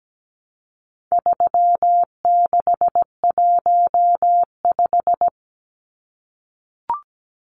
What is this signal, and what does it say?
Morse code, heard as a keyed tone, about 17 wpm, 700 Hz. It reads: 3615
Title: Numbers - 4-Digit 17wpm